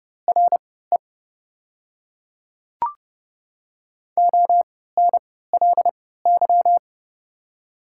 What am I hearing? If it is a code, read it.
Morse code, heard as a keyed tone, about 30 wpm, spaced as for 17 wpm, 690 Hz. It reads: RE ONLY
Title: 3x Character Spacing - Top 300 Words - 30wpm